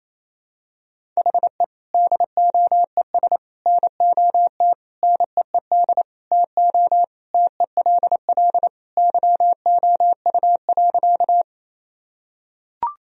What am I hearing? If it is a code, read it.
Morse code, heard as a keyed tone, about 28 wpm, 700 Hz. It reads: HE DOES NOT NEED TO TELL YOU.
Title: Sentences from Top 1000 Words - Review Part 1 - 28wpm